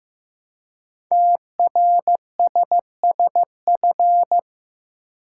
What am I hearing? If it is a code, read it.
Morse code, heard as a keyed tone, about 15 wpm, 700 Hz. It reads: TRSSF
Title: Code Groups - 5 Letters - 15wpm